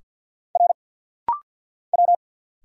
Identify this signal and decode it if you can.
Morse code, heard as a keyed tone, about 50 wpm, 690 Hz. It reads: R W
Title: Single Letter-Number Alt - Rapid-Fire 50wpm